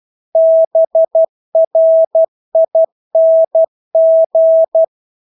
Morse code, heard as a keyed tone, about 12 wpm, 650 Hz. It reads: BRING
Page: Engelske ord 12 wpm